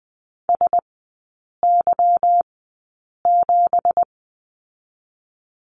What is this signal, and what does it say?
Morse code, heard as a keyed tone, about 20 wpm, 700 Hz. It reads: S Y 7
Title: 172 Lesson 28e - Number 9 - Three Characters 20wpm